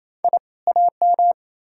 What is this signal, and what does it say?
Morse code, heard as a keyed tone, about 37 wpm, 700 Hz. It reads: I AM